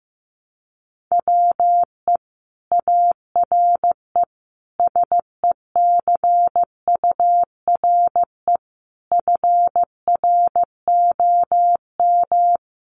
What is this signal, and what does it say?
Morse code, heard as a keyed tone, about 15 wpm, 700 Hz. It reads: WE ARE SECURE FROM
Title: Sentences from Top 1200 Words 15wpm